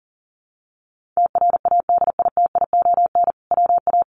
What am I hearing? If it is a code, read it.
Morse code, heard as a keyed tone, about 40 wpm, 700 Hz. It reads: TRADITION WA
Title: Sentences from Top 1500 Words 40wpm